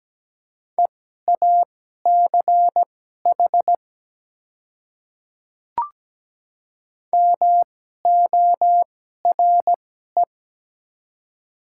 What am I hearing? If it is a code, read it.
Morse code, heard as a keyed tone, about 17 wpm, spaced as for 12 wpm, 700 Hz. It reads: EACH MORE
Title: 2x Character Spacing - Top 200 Words - 17wpm